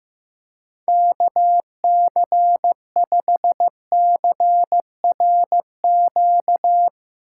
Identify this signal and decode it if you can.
Morse code, heard as a keyed tone, about 15 wpm, 700 Hz. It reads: KC5CRQ
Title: USA Call Signs - 6 Characters - 3x Character Spacing - 15wpm